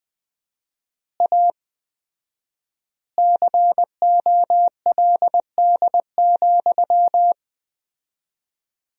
Morse code, heard as a keyed tone, about 20 wpm, 700 Hz. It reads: A COLD,
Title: 4x Word-Spacing - Sentences from Top 500 Words - 20wpm